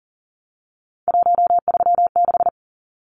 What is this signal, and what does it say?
Morse code, heard as a keyed tone, about 40 wpm, 700 Hz. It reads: 136
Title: Numbers - 3-Digit 40wpm